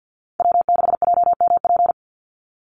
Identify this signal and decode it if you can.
Morse code, heard as a keyed tone, about 50 wpm, 700 Hz. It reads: W6JMP
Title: NA SOTA Chaser Callsigns 50wpm